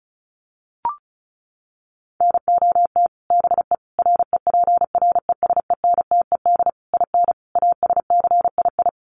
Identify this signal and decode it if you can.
Morse code, heard as a keyed tone, about 34 wpm, 700 Hz. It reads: NOT BE REPRESENTED IN ASCII
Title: Sets of 5 Words 35wpm